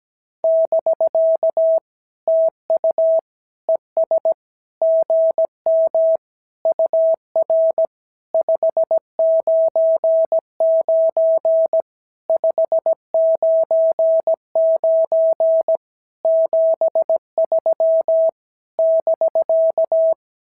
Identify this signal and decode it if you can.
Morse code, heard as a keyed tone, about 17 wpm, 650 Hz. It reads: <BK> TU ES GM UR 599 599 73 <BK>
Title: SOTA QSO Practice 17wpm